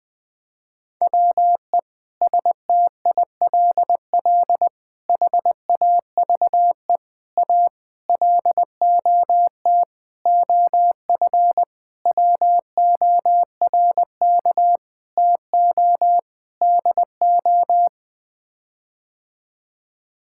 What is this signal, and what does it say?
Morse code, heard as a keyed tone, about 20 wpm, 700 Hz. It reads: WE STILL HAVE A LOT OF WORK TO DO
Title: Sets of 9 Words - 20wpm